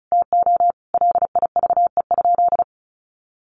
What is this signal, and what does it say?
Morse code, heard as a keyed tone, about 34 wpm, 700 Hz. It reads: TO LIVE?
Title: Sentences from Top 1500 Words 35wpm